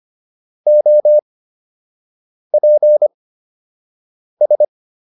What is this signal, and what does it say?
Morse code, heard as a keyed tone, about 25 wpm, spaced as for 6 wpm, 600 Hz. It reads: OPS